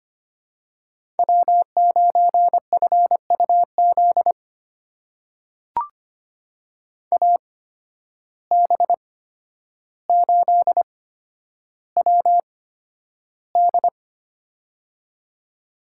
Morse code, heard as a keyed tone, about 25 wpm, spaced as for 11 wpm, 700 Hz. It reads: W9FUZ AB8WD
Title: USA Call Signs - 5 Characters - 8x Character Spacing - 25wpm